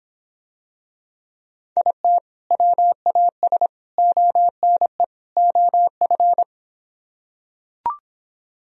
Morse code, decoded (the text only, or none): IT WAS ONE OF